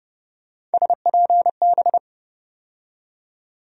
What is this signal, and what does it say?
Morse code, heard as a keyed tone, about 32 wpm, 700 Hz. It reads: SPB
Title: USA Call Sign Postfixes - 3 Letters 30wpm